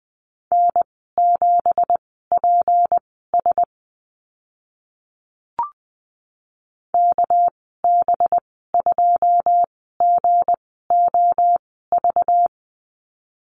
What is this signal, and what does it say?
Morse code, heard as a keyed tone, about 20 wpm, spaced as for 15 wpm, 700 Hz. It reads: N7PS KB2GOV
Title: 200 Lesson 32g - Letter k - Call Signs 20wpm